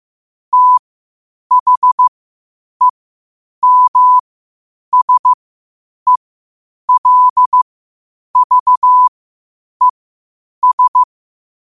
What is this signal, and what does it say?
Morse code, heard as a keyed tone, about 15 wpm, spaced as for 8 wpm, 1000 Hz. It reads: THEMSELVES
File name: qotdfiles / 8wpm